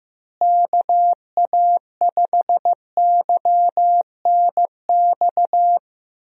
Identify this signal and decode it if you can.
Morse code, heard as a keyed tone, about 15 wpm, 700 Hz. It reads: KA5YNX
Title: USA Call Signs - 6 Characters - Repeated 2x - 15wpm